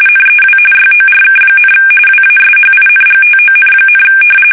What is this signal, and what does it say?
RTTY, about 45.45 baud. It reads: NEW YORK, WASHINGTON,
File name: fsk850 / battleshipnj